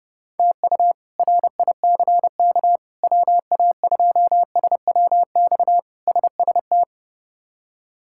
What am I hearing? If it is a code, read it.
Morse code, heard as a keyed tone, about 30 wpm, 700 Hz. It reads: TU RICK WA2SWX SST
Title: K1USN SST QSO Practice - 30wpm